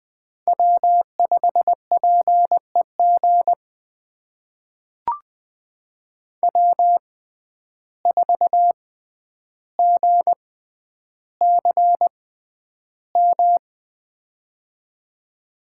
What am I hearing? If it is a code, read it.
Morse code, heard as a keyed tone, about 20 wpm, spaced as for 10 wpm, 700 Hz. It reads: W5PEG W4GCM
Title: USA Call Signs - 5 Characters - 6x Character Spacing - 20wpm